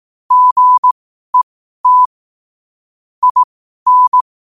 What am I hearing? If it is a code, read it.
Morse code, heard as a keyed tone, about 18 wpm, spaced as for 12 wpm, 1000 Hz. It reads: GET IN